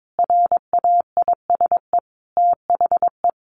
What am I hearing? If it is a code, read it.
Morse code, heard as a keyed tone, about 22 wpm, 700 Hz. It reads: RAISE THE
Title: A Sound Of Thunder 22wpm